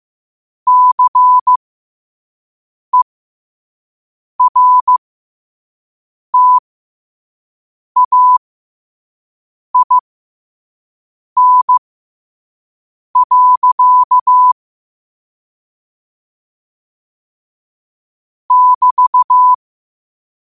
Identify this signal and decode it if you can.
Morse code, heard as a keyed tone, about 15 wpm, spaced as for 5 wpm, 1000 Hz. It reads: CERTAIN. <BT>
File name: qotdfiles / 5wpm